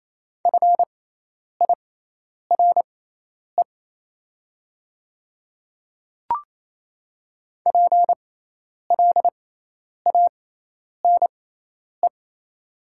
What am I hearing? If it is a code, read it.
Morse code, heard as a keyed tone, about 28 wpm, spaced as for 10 wpm, 690 Hz. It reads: FIRE PLANE
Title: 6x Character Spacing - Top 500 Words - 28wpm